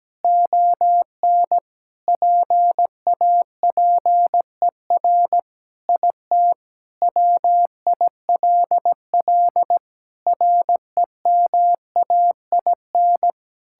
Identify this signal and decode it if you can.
Morse code, heard as a keyed tone, about 17 wpm, 700 Hz. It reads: ON PAPER IT WILL REMAIN
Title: Sentences from Top 1600 Words 17wpm